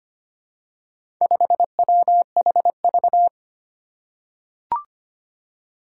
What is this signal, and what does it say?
Morse code, heard as a keyed tone, about 25 wpm, 700 Hz. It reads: 5WHV
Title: Four Challenging Letter-Number Combinations 25wpm